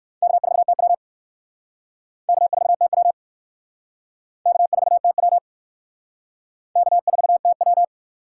Morse code, heard as a keyed tone, about 59 wpm, 700 Hz. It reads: K4TW K4TW K4TW K4TW